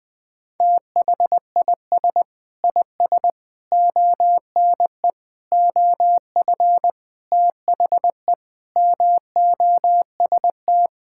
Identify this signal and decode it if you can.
Morse code, heard as a keyed tone, about 20 wpm, 700 Hz. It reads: THIS IS ONE OF THE MOST
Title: Sentences with 5-Word Phrases - Spoken Beginning - 20wpm